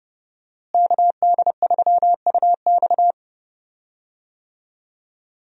Morse code, heard as a keyed tone, about 30 wpm, 700 Hz. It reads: KD3UX
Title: USA Call Signs - 5 Characters - Repeated 3x - 30wpm